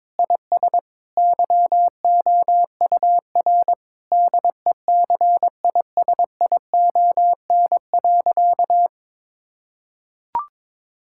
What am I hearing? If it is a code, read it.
Morse code, heard as a keyed tone, about 22 wpm, 700 Hz. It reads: IS YOUR DECISION.
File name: practice / 22wpm / Sentences with 8-Word Phrases - 22wpm